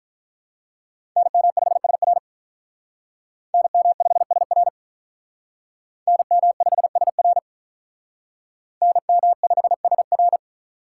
Morse code, heard as a keyed tone, about 43 wpm, 700 Hz. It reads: NM5SR NM5SR NM5SR NM5SR
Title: Speed Racing - USA Call Signs 35wpm